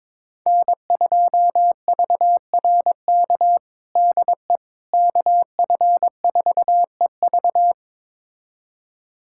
Morse code, heard as a keyed tone, about 22 wpm, 700 Hz. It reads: N2VRK DE KF4EV
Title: USA Callsign DE Callsign - 5 Characters - 22wpm